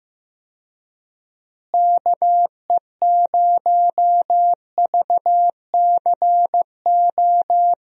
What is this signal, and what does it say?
Morse code, heard as a keyed tone, about 15 wpm, 700 Hz. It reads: KE0VCO
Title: USA Call Signs - 6 Characters - Repeated 2x - 15wpm